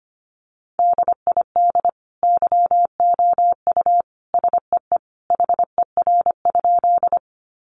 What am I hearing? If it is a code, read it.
Morse code, heard as a keyed tone, about 25 wpm, 700 Hz. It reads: DID YOU SEE HER?
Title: Sentences from Top 100 Words 25wpm